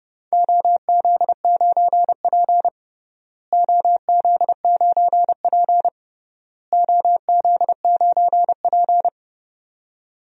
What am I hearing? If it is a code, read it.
Morse code, heard as a keyed tone, about 32 wpm, 700 Hz. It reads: OZ9P OZ9P OZ9P
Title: DXCC-100 Call Signs - 4 Characters - Repeated 3x - 30wpm